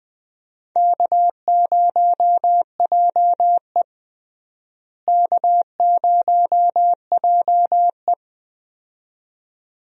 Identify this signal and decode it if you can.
Morse code, heard as a keyed tone, about 20 wpm, 700 Hz. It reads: K0JE K0JE
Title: USA Call Signs - 4 Characters - Repeated 2x - 20wpm